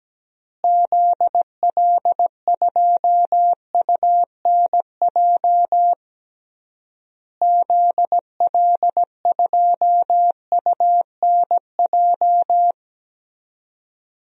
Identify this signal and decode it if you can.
Morse code, heard as a keyed tone, about 17 wpm, 700 Hz. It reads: ZL2UNJ ZL2UNJ
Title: DXCC-100 Call Signs - 6 Characters - Repeated 3x - 17wpm